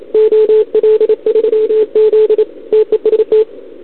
Morse code, heard as a keyed tone, about 27 wpm, 440 Hz. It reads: OL3Z TEST
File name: CQWWCW2021 / OL3Z